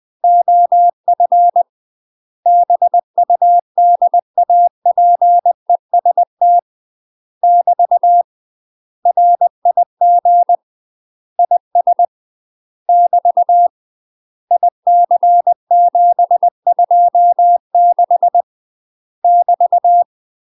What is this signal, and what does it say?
Morse code, heard as a keyed tone, about 20 wpm, 700 Hz. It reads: OF BUDAPEST <BT> RIG IS <BT> IC726 <BT>